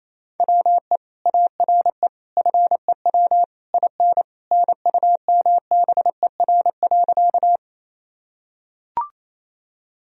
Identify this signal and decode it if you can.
Morse code, heard as a keyed tone, about 27 wpm, 700 Hz. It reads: WE ARE FEW IN NUMBER.
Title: Sentences from Top 1000 Words - Review Part 1 - 28wpm